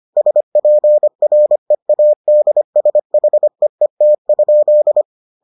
Morse code, heard as a keyed tone, about 25 wpm, 600 Hz. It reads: SPREADSHEET?